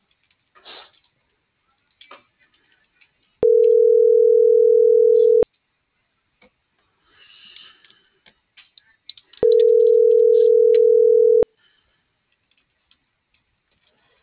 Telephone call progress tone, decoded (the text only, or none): ringback tone